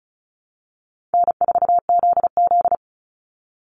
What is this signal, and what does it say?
Morse code, heard as a keyed tone, about 34 wpm, 700 Hz. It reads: N4ZZ
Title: Call Signs Repeated 3x - 35wpm